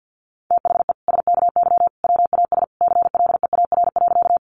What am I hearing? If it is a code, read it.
Morse code, heard as a keyed tone, about 50 wpm, 700 Hz. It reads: THE SKY WAS CLEAR.
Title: Sentences from Top 1200 Words 50wpm